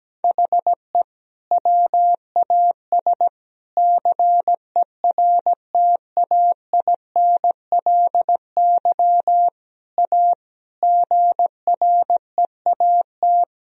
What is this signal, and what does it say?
Morse code, heard as a keyed tone, about 17 wpm, 700 Hz. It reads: HE WAS CERTAINLY A GREAT
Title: Sentences from Top 1000 Words - Review Part 1 - 17wpm